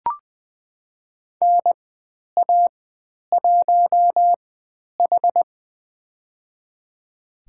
Morse code, heard as a keyed tone, about 20 wpm, spaced as for 10 wpm, 700 Hz. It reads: NA1H
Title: 102 Lesson 17g - Letter u - Call Signs 20wpm